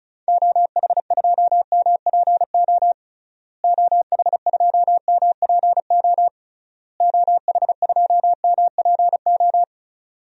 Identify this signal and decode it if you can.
Morse code, heard as a keyed tone, about 34 wpm, 700 Hz. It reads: OH2MPO OH2MPO OH2MPO
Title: DXCC-100 Call Signs - 6 Characters - Repeated 3x - 35wpm